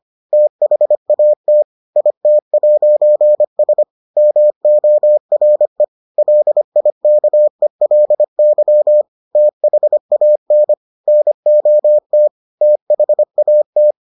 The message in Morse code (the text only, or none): THAT IT'S MORE LIKELY THAN NOT THAT